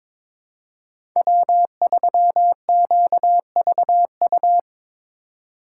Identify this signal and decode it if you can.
Morse code, heard as a keyed tone, about 22 wpm, 700 Hz. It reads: W3QVU
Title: USA Call Signs - 5 Characters - 3x Character Spacing - 22wpm